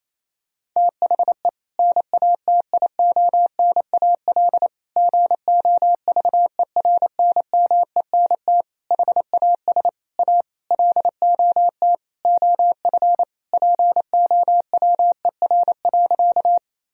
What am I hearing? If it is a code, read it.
Morse code, heard as a keyed tone, about 28 wpm, 700 Hz. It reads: THE NATIONAL GOVERNMENT HAS A LOT OF POWER.
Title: Sentences from Top 1000 Words - Review Part 4 - 28wpm